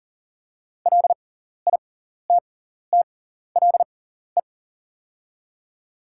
Morse code, heard as a keyed tone, about 40 wpm, spaced as for 14 wpm, 700 Hz. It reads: LITTLE
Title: 6x Character Spacing - Top 300 Words - 40wpm